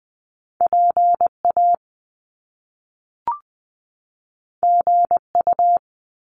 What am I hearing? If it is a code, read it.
Morse code, heard as a keyed tone, about 20 wpm, 700 Hz. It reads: PA GU
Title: Top Two Letter Combinations in Words 20wpm